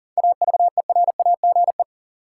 Morse code, heard as a keyed tone, about 40 wpm, 700 Hz. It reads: AVERAGE
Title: Seven Letter Words - 40wpm